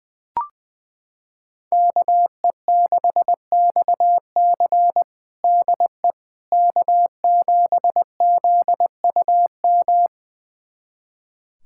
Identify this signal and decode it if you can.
Morse code, heard as a keyed tone, about 21 wpm, 700 Hz. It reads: KE6XC DE K7ZUM
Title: USA Callsign DE Callsign - 5 Characters - 20wpm